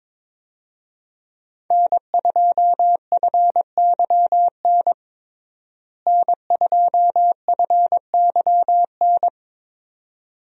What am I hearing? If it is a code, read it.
Morse code, heard as a keyed tone, about 22 wpm, 700 Hz. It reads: N2FYN N2FYN
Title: USA Call Signs - 5 Characters - Repeated 2x - 22wpm